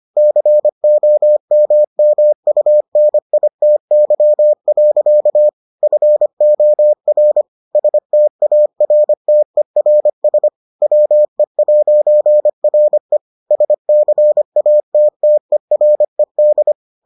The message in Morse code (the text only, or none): COMMUNITY. FOR STARTERS WE'RE SCATTERED